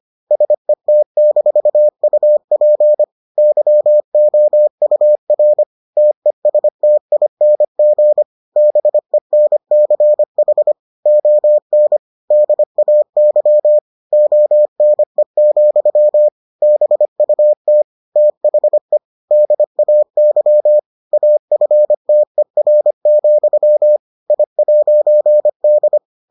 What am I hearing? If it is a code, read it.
Morse code, heard as a keyed tone, about 25 wpm, 600 Hz. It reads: SET-UP YOUR TESTING BENCH ON DAY ONE, BUT THE DAY AFTER, I'D